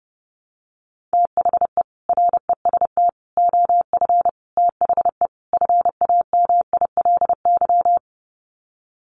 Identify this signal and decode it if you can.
Morse code, heard as a keyed tone, about 30 wpm, 700 Hz. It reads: THE REST OF THE FAMILY
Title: Sets of 5 Words Encore2 30wpm